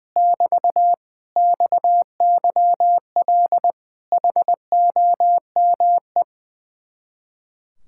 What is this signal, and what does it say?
Morse code, heard as a keyed tone, about 20 wpm, 700 Hz. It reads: <BT> XYL HOME
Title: Speed-Racing Common QSO Phrases Encore - 20wpm